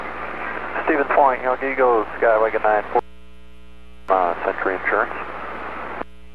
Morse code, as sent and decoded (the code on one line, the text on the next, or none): none